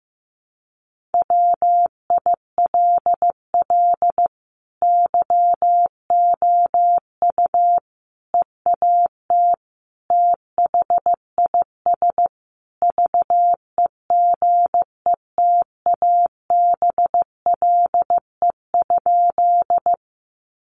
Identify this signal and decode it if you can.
Morse code, heard as a keyed tone, about 15 wpm, 700 Hz. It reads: WILL YOU EAT THIS VEGETABLE?
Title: Sentences from Top 1300 Words 15wpm